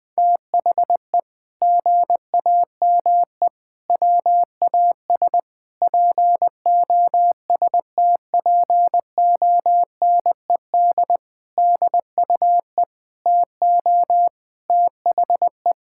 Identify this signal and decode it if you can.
Morse code, heard as a keyed tone, about 21 wpm, 700 Hz. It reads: THE GAME WAS POSTPONED DUE TO THE
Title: Sentences with 6-Word Phrases - 20wpm